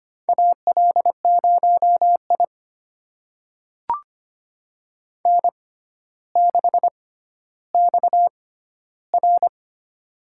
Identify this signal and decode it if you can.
Morse code, heard as a keyed tone, about 25 wpm, spaced as for 13 wpm, 700 Hz. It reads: AL0I N6XR
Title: USA Call Signs - 4 Characters - 6x Character Spacing - 25wpm